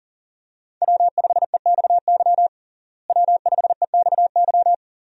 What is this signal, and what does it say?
Morse code, heard as a keyed tone, about 40 wpm, 700 Hz. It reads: W5EXY W5EXY
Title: USA Call Signs - 5 Characters - Repeated 2x - 40wpm